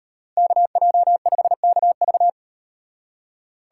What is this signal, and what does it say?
Morse code, heard as a keyed tone, about 38 wpm, 700 Hz. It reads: KJ5KV